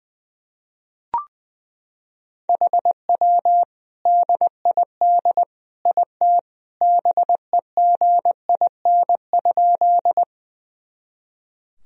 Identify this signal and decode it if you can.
Morse code, heard as a keyed tone, about 20 wpm, 700 Hz. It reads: HW DID IT BEGIN?
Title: Sentences from Top 300 Words and Abbreviations 20wpm